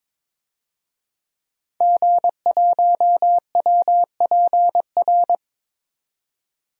Morse code, heard as a keyed tone, about 22 wpm, 700 Hz. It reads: G1WPR
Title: DXCC-100 Call Signs - 5 Characters - 22wpm